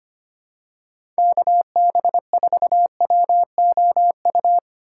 Morse code, heard as a keyed tone, about 25 wpm, 700 Hz. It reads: KB4WOU
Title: USA Call Signs - 4x Character Spacing - 25wpm